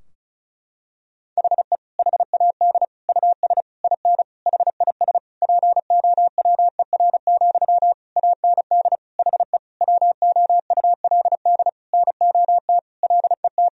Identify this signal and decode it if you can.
Morse code, heard as a keyed tone, about 34 wpm, 700 Hz. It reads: HE HAD US IN HIS POWER, AND HE WOULD NOT LET
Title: Sentences from Top 1000 Words - Review Part 2 - 35wpm